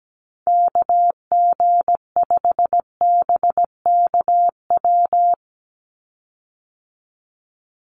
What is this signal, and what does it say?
Morse code, heard as a keyed tone, about 17 wpm, 700 Hz. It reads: KG5BKW
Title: USA Call Signs 17wpm